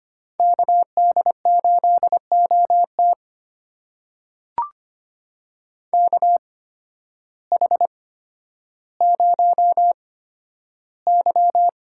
Morse code, heard as a keyed tone, about 25 wpm, spaced as for 12 wpm, 700 Hz. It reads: KD8OT KH0Y
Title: USA Call Signs - 5 Characters - 8x Character Spacing - 25wpm